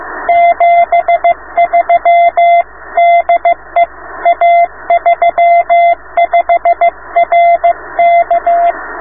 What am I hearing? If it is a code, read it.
Morse code, heard as a keyed tone, about 15 wpm, 690 Hz. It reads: 73 DE A35RK